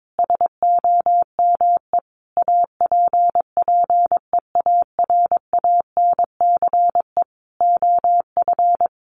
Morse code, heard as a keyed tone, about 22 wpm, 700 Hz. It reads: SOME APPEARANCE OF
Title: Sentences from Top 1100 Words 22wpm